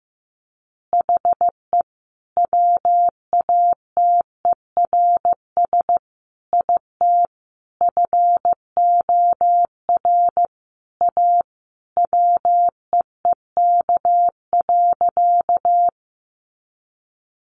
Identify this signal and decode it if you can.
Morse code, heard as a keyed tone, about 15 wpm, 700 Hz. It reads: HE WATERS IT FOR A WEEK.
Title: Easy Phrases 15wpm